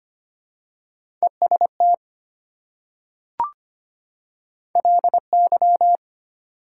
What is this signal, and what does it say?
Morse code, heard as a keyed tone, about 25 wpm, 700 Hz. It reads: EST LY
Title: Prefixes and Suffixes 25wpm